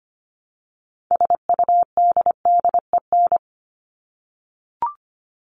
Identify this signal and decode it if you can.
Morse code, heard as a keyed tone, about 25 wpm, 700 Hz. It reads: SUDDEN
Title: LY Suffix - Top 100 Words with Base Words 25wpm